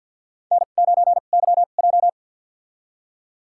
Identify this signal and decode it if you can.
Morse code, heard as a keyed tone, about 51 wpm, spaced as for 36 wpm, 700 Hz. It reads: N9YJ
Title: USA Call Signs - 4 Characters - 2x Character Spacing - 50wpm